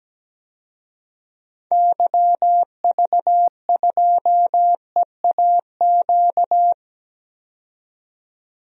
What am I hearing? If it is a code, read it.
Morse code, heard as a keyed tone, about 17 wpm, 700 Hz. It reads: YV2EAQ
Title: DXCC-100 Call Signs - 6 Characters - 17wpm